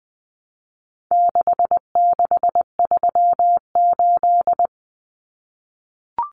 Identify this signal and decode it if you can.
Morse code, heard as a keyed tone, about 20 wpm, 700 Hz. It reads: 6638
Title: Numbers - 4-Digit 20wpm